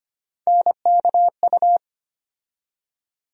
Morse code, heard as a keyed tone, about 25 wpm, 700 Hz. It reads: NKU
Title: USA Call Sign Postfixes - 3 Letters 25wpm